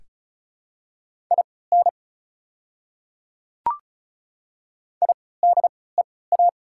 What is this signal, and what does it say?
Morse code, heard as a keyed tone, about 36 wpm, spaced as for 20 wpm, 700 Hz. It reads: IN IDEA
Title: 3x Character Spacing - Top 300 Words - 35wpm